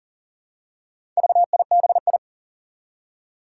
Morse code, heard as a keyed tone, about 40 wpm, 700 Hz. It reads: VIBI
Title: Four Challenging Letter-Number Combinations 40wpm